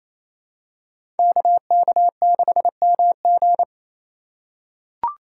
Morse code, heard as a keyed tone, about 28 wpm, 700 Hz. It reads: KK6MG
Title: USA Call Signs - 5 Characters - 2x Character Spacing - 28wpm